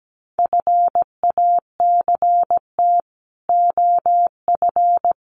Morse code, heard as a keyed tone, about 17 wpm, 700 Hz. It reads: FACT OF
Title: Sets of 5 Words 17wpm